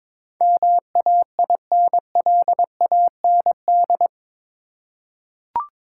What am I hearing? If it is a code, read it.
Morse code, heard as a keyed tone, about 22 wpm, 700 Hz. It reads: MAINLAND
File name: practice / 22wpm / Eight Letter Words - 22wpm